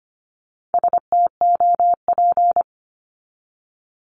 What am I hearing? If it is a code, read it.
Morse code, heard as a keyed tone, about 25 wpm, 700 Hz. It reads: STOP
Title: Top 500 Words 25wpm